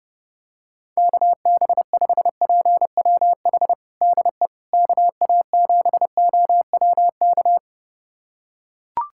Morse code, heard as a keyed tone, about 30 wpm, 700 Hz. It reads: KB5PWH DE KA7OWK
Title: USA Callsign DE Callsign 30wpm